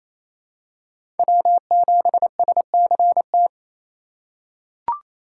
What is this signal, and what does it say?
Morse code, heard as a keyed tone, about 28 wpm, 700 Hz. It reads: W7SCT
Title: USA Call Signs - 5 Characters - 2x Character Spacing - 28wpm